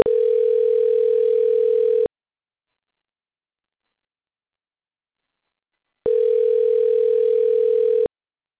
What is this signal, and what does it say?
Ringback tone (North America), 440 + 480 Hz.